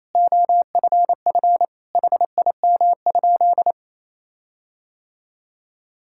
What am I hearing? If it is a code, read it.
Morse code, heard as a keyed tone, about 28 wpm, 700 Hz. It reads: OFF HIM?
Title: Sentences from Top 1600 Words 28wpm